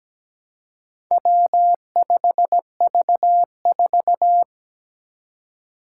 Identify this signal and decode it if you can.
Morse code, heard as a keyed tone, about 17 wpm, 700 Hz. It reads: W5V4
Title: Four Challenging Letter-Number Combinations 17wpm